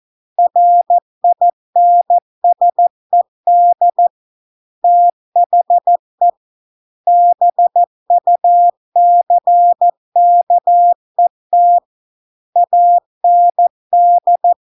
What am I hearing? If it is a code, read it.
Morse code, heard as a keyed tone, about 14 wpm, 700 Hz. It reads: RINSED THE BUCKET AND